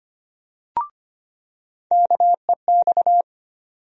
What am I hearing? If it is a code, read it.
Morse code, heard as a keyed tone, about 25 wpm, 700 Hz. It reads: KEX
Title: USA Call Sign Postfixes - 3 Letters 25wpm